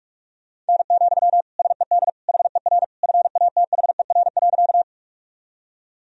Morse code, heard as a keyed tone, about 41 wpm, 700 Hz. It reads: N, SED HER FATHER.
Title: Sentences from Top 300 Words and Abbreviations 45wpm